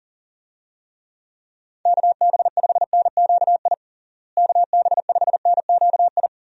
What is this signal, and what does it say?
Morse code, heard as a keyed tone, about 40 wpm, 700 Hz. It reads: KB5NQI KB5NQI
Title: USA Call Signs - 6 Characters - Repeated 2x - 40wpm